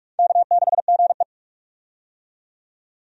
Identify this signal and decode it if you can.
Morse code, heard as a keyed tone, about 42 wpm, 700 Hz. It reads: K6GE
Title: USA Call Signs - 4 Characters - 45wpm